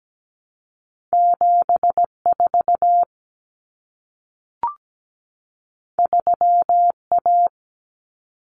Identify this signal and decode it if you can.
Morse code, heard as a keyed tone, about 17 wpm, 700 Hz. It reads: 74 3A
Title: Common Cut Numbers - 2-Digit 17wpm